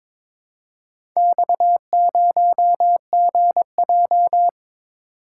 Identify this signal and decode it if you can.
Morse code, heard as a keyed tone, about 22 wpm, 690 Hz. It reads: X0GJ
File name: practice / 22wpm / Four Letter-Number Combinations - 22wpm